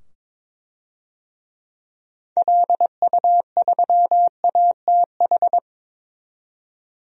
Morse code, heard as a keyed tone, about 21 wpm, 700 Hz. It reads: LU3ATH
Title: DXCC-100 Call Signs - 6 Characters - 22wpm